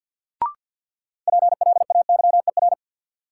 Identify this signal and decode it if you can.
Morse code, heard as a keyed tone, about 47 wpm, 700 Hz. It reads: PLAYER
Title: Sentences from Top 1000 Words - Review Part 4 - New Word Drill - 50wpm